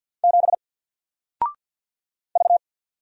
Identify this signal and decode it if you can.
Morse code, heard as a keyed tone, about 48 wpm, 700 Hz. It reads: 7 V
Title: Single Character (Letter:Number:Punctuation:Prosign) 50wpm